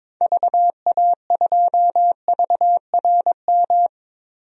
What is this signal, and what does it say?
Morse code, heard as a keyed tone, about 22 wpm, 700 Hz. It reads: VA2VRM
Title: DXCC-100 Call Signs - 6 Characters - 22wpm